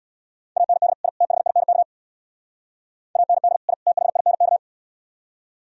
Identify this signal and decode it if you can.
Morse code, heard as a keyed tone, about 76 wpm, 700 Hz. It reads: AND I THINK AND I THINK